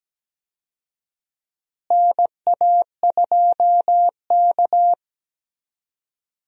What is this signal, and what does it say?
Morse code, heard as a keyed tone, about 17 wpm, 700 Hz. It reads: NA2K
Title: USA Call Signs - 4 Characters - Repeated 2x - 17wpm